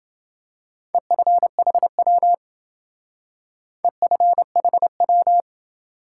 Morse code, heard as a keyed tone, about 28 wpm, 700 Hz. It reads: EFHW EFHW